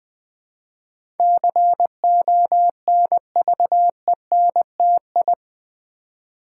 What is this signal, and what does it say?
Morse code, heard as a keyed tone, about 20 wpm, 700 Hz. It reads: CONVENTI
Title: Build A Word - 4 Letters At A Time - 12-Letter Words - 20wpm